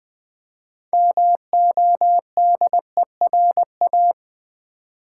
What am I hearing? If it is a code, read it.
Morse code, heard as a keyed tone, about 20 wpm, 700 Hz. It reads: MODERA
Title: Build A Word - 2 Letters At A Time - 8-Letter Words - 20wpm